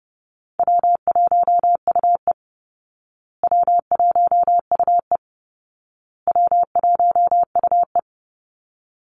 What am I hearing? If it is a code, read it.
Morse code, heard as a keyed tone, about 30 wpm, 700 Hz. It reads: W1UE W1UE W1UE
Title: Easy Call Signs Repeated 3x - 30wpm